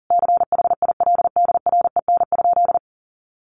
Morse code, heard as a keyed tone, about 40 wpm, 700 Hz. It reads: CHILDREN?
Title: Sentences from Top 1500 Words 40wpm